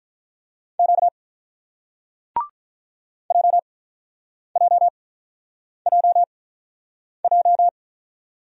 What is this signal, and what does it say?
Morse code, heard as a keyed tone, about 48 wpm, spaced as for 9 wpm, 700 Hz. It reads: Q JJJJ